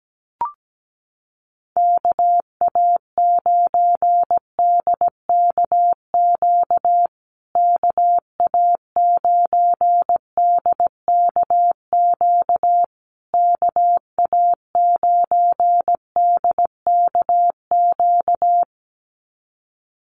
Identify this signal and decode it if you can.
Morse code, heard as a keyed tone, about 17 wpm, 700 Hz. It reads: KA9DKQ KA9DKQ KA9DKQ
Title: USA Call Signs Repeated 3x 17wpm